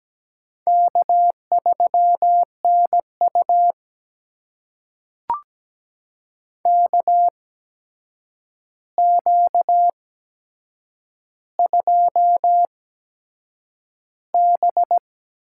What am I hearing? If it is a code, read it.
Morse code, heard as a keyed tone, about 17 wpm, spaced as for 7 wpm, 700 Hz. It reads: K3NU KQ2B
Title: USA Call Signs - 4 Characters - 8x Character Spacing - 17wpm